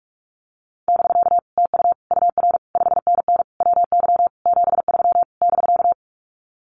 Morse code, heard as a keyed tone, about 44 wpm, 700 Hz. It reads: <BK> TU UR 5NN WY 73 <BK>
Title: Common QSO Phrases Encore 45wpm